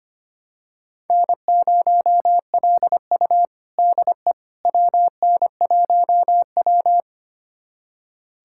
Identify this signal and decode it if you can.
Morse code, heard as a keyed tone, about 25 wpm, 700 Hz. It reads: N0LU DE WN1W
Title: USA Callsign DE Callsign - 4 Characters - 25wpm